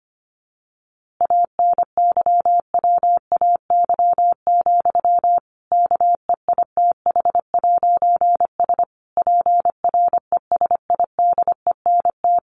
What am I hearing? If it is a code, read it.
Morse code, heard as a keyed tone, about 25 wpm, 700 Hz. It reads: ANYWAY, KEITH'S PRESIDENT
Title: A Sound Of Thunder 25wpm